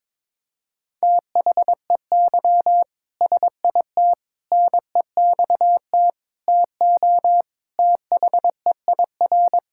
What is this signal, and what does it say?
Morse code, heard as a keyed tone, about 22 wpm, 700 Hz. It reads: THEY SIT NEXT TO THEIR
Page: morse-code-ninja-practice-22wpm